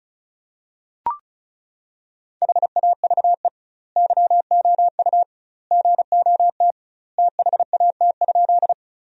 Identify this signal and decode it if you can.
Morse code, heard as a keyed tone, about 36 wpm, 700 Hz. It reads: HAVE YOU GOT THAT?
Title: 2x Word-Spacing - Sentences from Top 300 Words - 35wpm